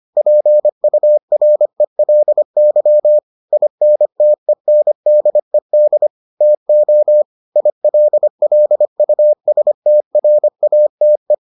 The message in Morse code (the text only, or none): PURELY INTENDED TO ILLUSTRATE